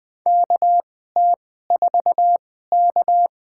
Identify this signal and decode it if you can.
Morse code, heard as a keyed tone, about 20 wpm, spaced as for 14 wpm, 700 Hz. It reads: KT4K
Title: USA Call Signs - 4 Characters - 2x Character Spacing - 20wpm